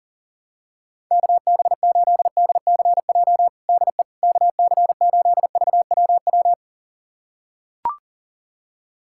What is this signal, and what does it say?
Morse code, heard as a keyed tone, about 40 wpm, 700 Hz. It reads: KB8DCJ DE KC8VWW
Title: USA Callsign DE Callsign - 6 Characters - 40wpm